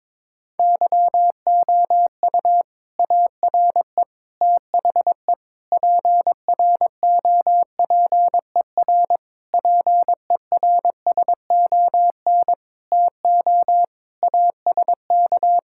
Morse code, heard as a keyed tone, about 22 wpm, 700 Hz. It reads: YOU ARE THE PROPER PERSON TO ASK
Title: Sentences from Top 1000 Words - Review Part 5 - 22wpm